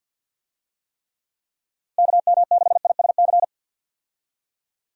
Morse code, heard as a keyed tone, about 48 wpm, 700 Hz. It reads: KM6ISC
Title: USA Call Signs - 6 Characters - Repeated 3x - 50wpm